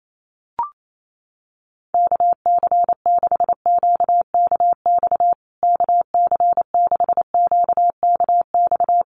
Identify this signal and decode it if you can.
Morse code, heard as a keyed tone, about 28 wpm, 690 Hz. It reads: KC6QKX KC6QKX
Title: USA Call Signs Repeated 3x 28wpm